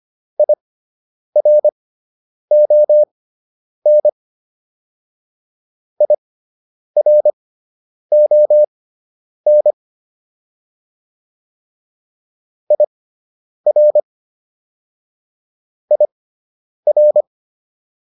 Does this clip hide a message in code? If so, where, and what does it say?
Yes: Morse code from 0.4 to 17.2 s, heard as a keyed tone, about 25 wpm, spaced as for 9 wpm, 600 Hz. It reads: IRON IRON IR IR